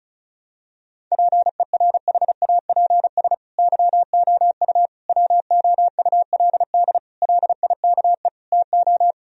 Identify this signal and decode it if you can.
Morse code, heard as a keyed tone, about 36 wpm, 700 Hz. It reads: PERHAPS YOU WOULD LIKE TO
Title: Sentences from Top 1000 Words - Review Part 2 - 35wpm